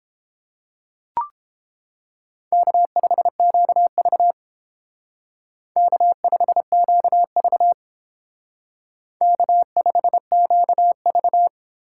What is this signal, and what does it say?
Morse code, heard as a keyed tone, about 31 wpm, 700 Hz. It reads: K5QV K5QV K5QV